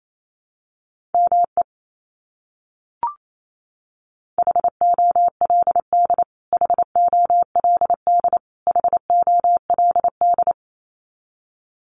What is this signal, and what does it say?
Morse code, heard as a keyed tone, about 28 wpm, 700 Hz. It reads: ME HOLD HOLD HOLD
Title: Top 500 Words Repeated 28wpm